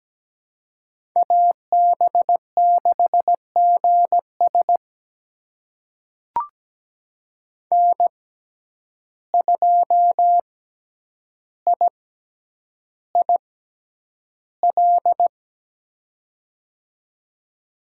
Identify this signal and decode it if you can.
Morse code, heard as a keyed tone, about 17 wpm, spaced as for 9 wpm, 700 Hz. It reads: AB6GS N2IIL
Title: USA Call Signs - 5 Characters - 6x Character Spacing - 17wpm